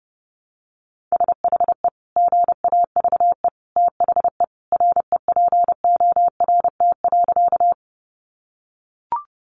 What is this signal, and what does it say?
Morse code, heard as a keyed tone, about 30 wpm, 700 Hz. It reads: SHE GAVE THE REPORT.
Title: Sentences from Top 900 Words 30wpm